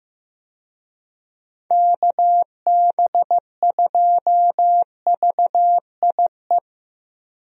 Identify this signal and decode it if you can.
Morse code, heard as a keyed tone, about 15 wpm, 700 Hz. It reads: KB2VIE
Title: USA Call Signs - 6 Characters - Repeated 2x - 15wpm